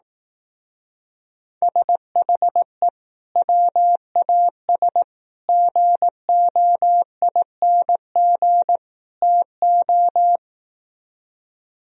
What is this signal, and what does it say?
Morse code, heard as a keyed tone, about 18 wpm, 700 Hz. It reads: SHE WAS GOING TO
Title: Speed-Racing - Sets of 4 Words 15wpm